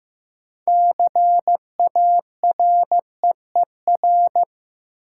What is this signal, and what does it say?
Morse code, heard as a keyed tone, about 15 wpm, 700 Hz. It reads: CAREER
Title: Sentences from Top 1000 Words - New Word Drill - 15wpm